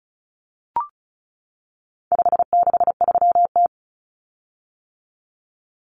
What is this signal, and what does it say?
Morse code, heard as a keyed tone, about 34 wpm, 700 Hz. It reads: 563T
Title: Common Cut Numbers - 4-Digit 35wpm